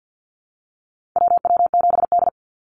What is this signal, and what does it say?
Morse code, heard as a keyed tone, about 48 wpm, 700 Hz. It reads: WW7D
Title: NA SOTA Chaser Callsigns 50wpm